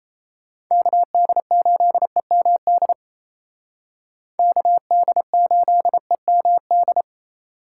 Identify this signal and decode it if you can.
Morse code, heard as a keyed tone, about 35 wpm, 700 Hz. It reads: KD8EMD KD8EMD